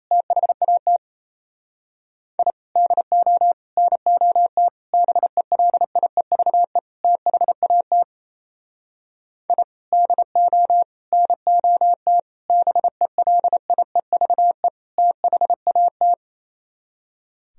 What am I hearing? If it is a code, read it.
Morse code, heard as a keyed tone, about 34 wpm, 700 Hz. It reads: THAT I DO NOT BELIEVE THAT I DO NOT BELIEVE THAT
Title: Speed-Racing - Sets of 5 Words 28wpm